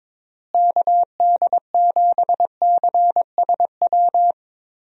Morse code, heard as a keyed tone, about 21 wpm, 700 Hz. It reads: KD7CSW
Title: USA Call Signs - 6 Characters - Repeated 3x - 22wpm